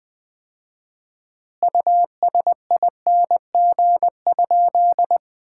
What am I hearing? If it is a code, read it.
Morse code, heard as a keyed tone, about 20 wpm, 700 Hz. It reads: USING?
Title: Common QSO Phrases - 4x Word-Spacing - 20wpm